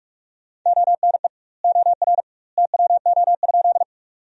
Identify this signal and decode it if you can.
Morse code, heard as a keyed tone, about 44 wpm, 700 Hz. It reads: ONE OR TWO?
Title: 2x Word-Spacing - Sentences from Top 100 Words - 45wpm